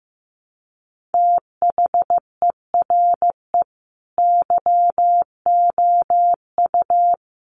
Morse code, heard as a keyed tone, about 15 wpm, 700 Hz. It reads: THERE YOU
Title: Sentences from Top 100 Words 15wpm